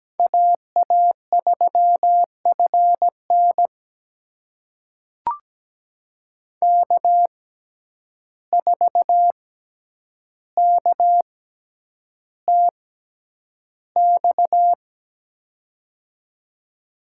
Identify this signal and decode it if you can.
Morse code, heard as a keyed tone, about 17 wpm, spaced as for 9 wpm, 700 Hz. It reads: AA3FN K4KTX
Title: USA Call Signs - 5 Characters - 6x Character Spacing - 17wpm